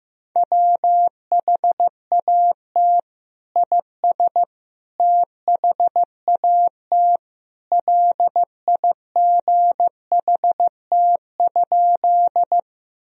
Morse code, heard as a keyed tone, about 15 wpm, 700 Hz. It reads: WHAT IS THAT LIGHT?
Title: Sentences from Top 1000 Words - Review Part 1 - 15wpm